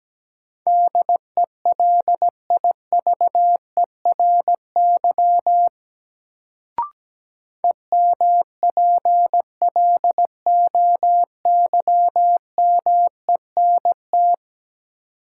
Morse code, heard as a keyed tone, about 17 wpm, 700 Hz. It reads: DELIVERY EMPLOYMENT
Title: Sentences from Top 1500 Words - New Word Drill - 17wpm